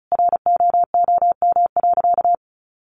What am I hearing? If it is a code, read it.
Morse code, heard as a keyed tone, about 34 wpm, 700 Hz. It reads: ROOM.
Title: A Sound Of Thunder 35wpm